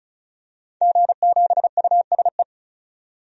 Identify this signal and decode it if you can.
Morse code, heard as a keyed tone, about 36 wpm, 700 Hz. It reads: G7USE
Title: DXCC-100 Call Signs - 5 Characters - 35wpm